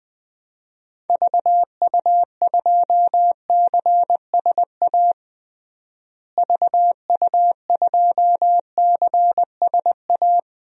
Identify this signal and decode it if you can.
Morse code, heard as a keyed tone, about 20 wpm, 700 Hz. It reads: VU2CSA VU2CSA
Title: DXCC-100 Call Signs - 6 Characters - Repeated 2x - 20wpm